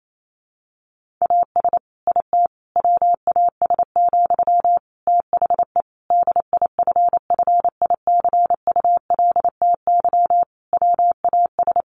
Morse code, heard as a keyed tone, about 28 wpm, 700 Hz. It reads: AS IT WAS, THE DIFFICULTY WAS
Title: Sentences from Top 1500 Words 28wpm